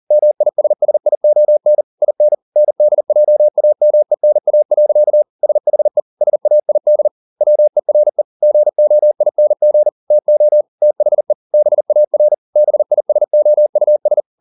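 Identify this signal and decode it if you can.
Morse code, heard as a keyed tone, about 40 wpm, 600 Hz. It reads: MISSION IN NDJAMENA. SHE SAID WERE GOING TO THE BAR BISOUS
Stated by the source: Morse code at 40 wpm, practice text, CW med Gnister.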